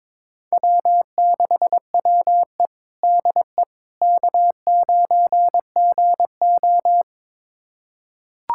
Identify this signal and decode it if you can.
Morse code, heard as a keyed tone, about 22 wpm, 700 Hz. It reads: W6WE DE K9GO
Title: USA Callsign DE Callsign - 4 Characters - 22wpm